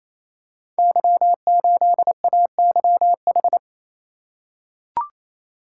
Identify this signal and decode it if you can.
Morse code, heard as a keyed tone, about 27 wpm, 700 Hz. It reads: Y8AYH
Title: Code Groups - 5 Letter:Numbers - 28wpm